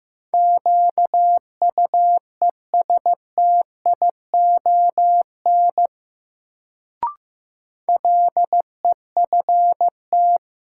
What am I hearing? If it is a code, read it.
Morse code, heard as a keyed tone, about 15 wpm, 700 Hz. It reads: QUESTION LEFT
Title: Sets of 5 Words Encore - New Word Drill - 15wpm